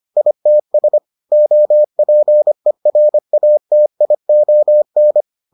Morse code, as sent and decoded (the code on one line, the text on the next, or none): .. - ... / --- .--. .-. .- - .. --- -.
ITS OPERATION